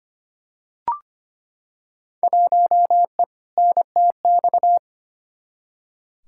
Morse code, heard as a keyed tone, about 25 wpm, 700 Hz. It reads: 1E NTX
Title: Field Day Exchanges - 25wpm